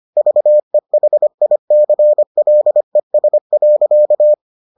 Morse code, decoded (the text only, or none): VEHICLES.